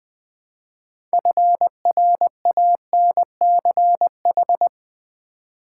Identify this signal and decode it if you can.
Morse code, heard as a keyed tone, about 20 wpm, 700 Hz. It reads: FRANCH
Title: Build A Word - 3 Letters At A Time - 9-Letter Words - 20wpm